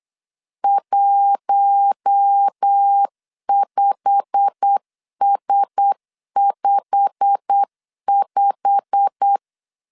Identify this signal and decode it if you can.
Morse code, heard as a keyed tone, about 8 wpm, 800 Hz. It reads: 15S55
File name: LEZIONI / 8 YJ45 / 8 YJ45 8wpm 3214